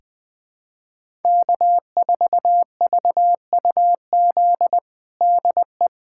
Morse code, heard as a keyed tone, about 21 wpm, 700 Hz. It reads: K4VUZ DE
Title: USA Callsign DE Callsign 20wpm